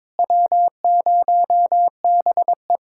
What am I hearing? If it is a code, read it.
Morse code, heard as a keyed tone, about 22 wpm, 700 Hz. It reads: W0BE
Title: Speed-Racing Common QSO Phrases Encore - 22wpm